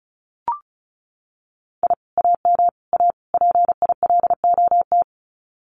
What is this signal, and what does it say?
Morse code, heard as a keyed tone, about 34 wpm, 700 Hz. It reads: I AM A PILOT
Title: Common QSO Phrases Encore 35wpm